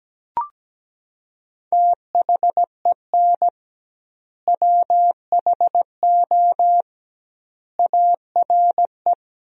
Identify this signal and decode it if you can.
Morse code, heard as a keyed tone, about 17 wpm, 700 Hz. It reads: THEN WHO ARE
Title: 2x Word-Spacing - Sentences from Top 100 Words - 17wpm